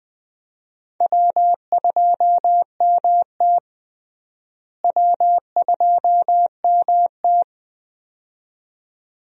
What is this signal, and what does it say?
Morse code, heard as a keyed tone, about 21 wpm, 700 Hz. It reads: W2MT W2MT
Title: USA Call Signs - 4 Characters - Repeated 2x - 20wpm